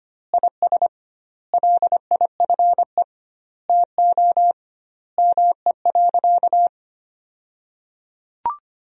Morse code, heard as a keyed tone, about 25 wpm, 700 Hz. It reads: IS LIFE TO ME.
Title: 2x Word-Spacing - Sentences from Top 500 Words - 25wpm